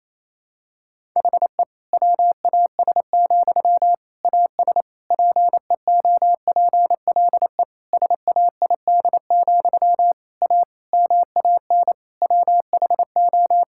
Morse code, heard as a keyed tone, about 28 wpm, 700 Hz. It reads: HE WAS, AS PEOPLE SAID, A MAN WHO
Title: Sentences from Top 1000 Words - Review Part 6 - 28wpm